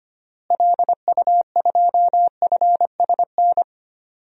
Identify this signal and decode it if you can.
Morse code, heard as a keyed tone, about 25 wpm, 700 Hz. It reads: LU2FSN
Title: DXCC-100 Call Signs - 6 Characters - Repeated 3x - 25wpm